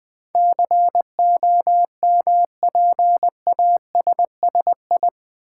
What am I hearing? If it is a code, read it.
Morse code, heard as a keyed tone, about 20 wpm, 700 Hz. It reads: COMPASSI
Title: Build A Word - 4 Letters At A Time - 10-Letter Words - 20wpm